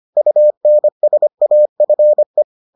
Morse code, heard as a keyed tone, about 25 wpm, 600 Hz. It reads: UNSAFE